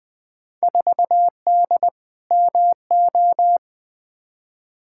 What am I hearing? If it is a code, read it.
Morse code, heard as a keyed tone, about 21 wpm, 700 Hz. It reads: 4D MO
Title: Field Day Exchanges Encore - 20wpm